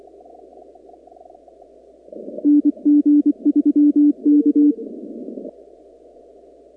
Morse code, heard as a keyed tone, about 24 wpm, 290 Hz. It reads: NG3K
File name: contests / 2019-cw-wpx-kt5j-k5pi- -2330 0-44